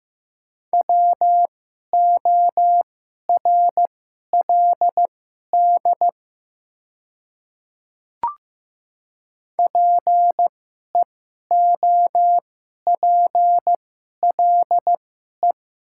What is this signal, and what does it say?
Morse code, heard as a keyed tone, about 15 wpm, spaced as for 11 wpm, 700 Hz. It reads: WORLD PEOPLE
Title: 2x Character Spacing - Top 200 Words - 15wpm